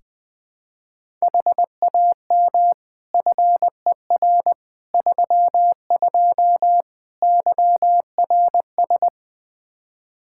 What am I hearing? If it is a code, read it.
Morse code, heard as a keyed tone, about 20 wpm, 700 Hz. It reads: HAM FER 32 YRS
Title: Practice QSO - Exchange 3 - More Info - 20wpm